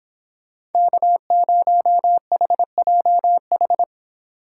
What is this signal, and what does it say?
Morse code, heard as a keyed tone, about 30 wpm, 700 Hz. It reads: K0HJH